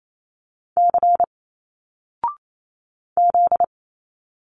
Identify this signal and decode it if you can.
Morse code, heard as a keyed tone, about 28 wpm, 700 Hz. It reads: C Z
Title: Single Letters 28wpm